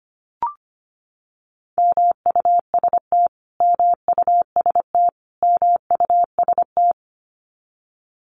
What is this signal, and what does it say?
Morse code, heard as a keyed tone, about 25 wpm, 700 Hz. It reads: MUST MUST MUST
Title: Top 200 Words Repeated 25wpm